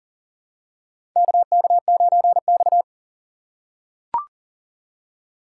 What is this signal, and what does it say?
Morse code, heard as a keyed tone, about 40 wpm, 700 Hz. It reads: KK9X
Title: USA Call Signs - 4 Characters - 6x Character Spacing - 40wpm